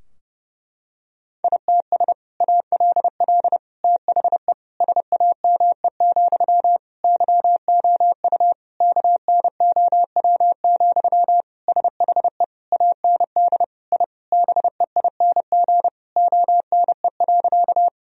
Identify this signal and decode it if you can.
Morse code, heard as a keyed tone, about 29 wpm, 700 Hz. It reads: ITS ALL THE SAME, YOU KNOW, SHE AND I BEING ONE.
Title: Sentences from Top 1000 Words - Review Part 3 - 30wpm